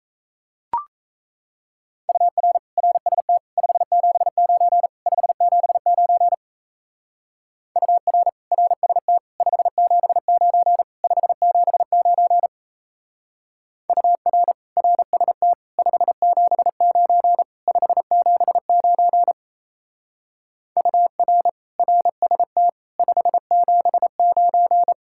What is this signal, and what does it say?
Morse code, heard as a keyed tone, about 33 wpm, 700 Hz. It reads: UR RST 579 579 UR RST 579 579 UR RST 579 579 UR RST 579
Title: Speed-Racing Common QSO Phrases - 28wpm